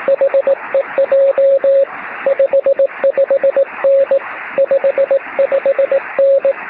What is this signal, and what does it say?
Morse code, heard as a keyed tone, about 18 wpm, 550 Hz, with radio noise. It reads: HEJ 55N 55N
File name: sm7lcb / 09-sm7ecm 23cm